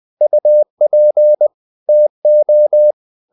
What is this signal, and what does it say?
Morse code, heard as a keyed tone, about 20 wpm, 600 Hz. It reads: UP TO
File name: the old man and the sea / the old man and the sea part 1 20wpm